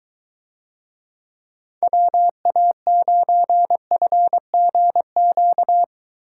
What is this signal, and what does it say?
Morse code, heard as a keyed tone, about 23 wpm, 700 Hz. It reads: WA9FGQ